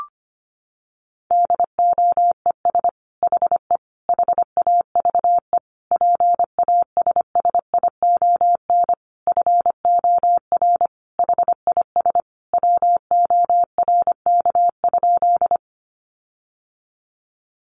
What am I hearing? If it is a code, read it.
Morse code, heard as a keyed tone, about 25 wpm, 700 Hz. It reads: DOES HE HAVE PASSION FOR HIS WORK?
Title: Sentences from Top 1300 Words 25wpm